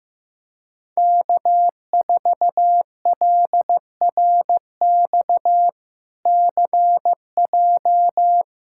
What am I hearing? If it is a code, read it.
Morse code, heard as a keyed tone, about 15 wpm, 700 Hz. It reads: K4LRX CJ
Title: K1USN SST QSO Practice - 15wpm